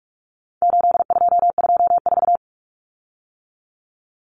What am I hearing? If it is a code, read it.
Morse code, heard as a keyed tone, about 44 wpm, 700 Hz. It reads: 8224
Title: Common Cut Numbers - 4-Digit 45wpm